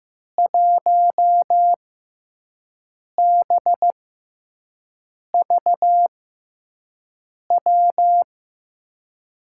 Morse code, heard as a keyed tone, about 15 wpm, spaced as for 5 wpm, 700 Hz. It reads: 1BVW
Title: USA Call Signs - 6 Characters - 6x Character Spacing - 15wpm